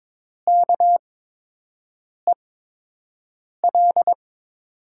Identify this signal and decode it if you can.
Morse code, heard as a keyed tone, about 22 wpm, spaced as for 6 wpm, 700 Hz. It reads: KEL
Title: USA Call Signs - 6 Characters - 8x Character Spacing - 22wpm